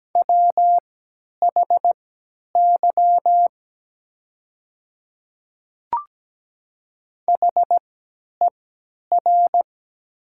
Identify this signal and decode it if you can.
Morse code, heard as a keyed tone, about 17 wpm, spaced as for 10 wpm, 700 Hz. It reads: WHY HER
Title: 3x Character Spacing - Top 200 Words - 17wpm